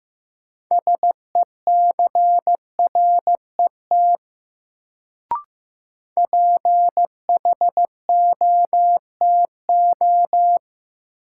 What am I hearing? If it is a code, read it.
Morse code, heard as a keyed tone, about 15 wpm, 700 Hz. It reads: SECRET PHOTO
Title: Sentences from Top 1200 Words - New Word Drill Alt - 15wpm